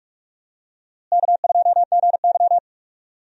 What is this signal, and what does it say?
Morse code, heard as a keyed tone, about 42 wpm, 700 Hz. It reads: K2GY
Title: USA Call Signs - 4 Characters - Repeated 2x - 45wpm